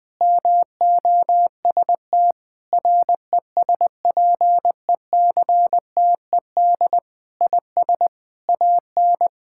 Morse code, heard as a keyed tone, about 21 wpm, 700 Hz. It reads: MOST RESPECTED IS AN
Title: Sentences with 6-Word Phrases - Spoken Beginning - 20wpm